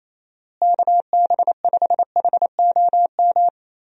Morse code, heard as a keyed tone, about 27 wpm, 700 Hz. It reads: KB5HOM
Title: USA Call Signs - 6 Characters - Repeated 3x - 28wpm